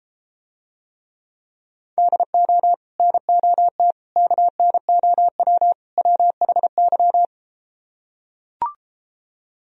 Morse code, heard as a keyed tone, about 33 wpm, 700 Hz. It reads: DO NOT KNOW WHY